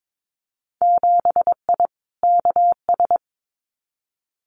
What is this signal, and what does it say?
Morse code, heard as a keyed tone, about 22 wpm, 700 Hz. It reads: 7I KS
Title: Winter Field Day Exchanges Encore 22wpm